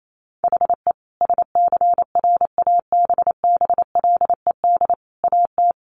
Morse code, heard as a keyed tone, about 28 wpm, 700 Hz. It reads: HE SCRABBLED AT
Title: A Sound Of Thunder 28wpm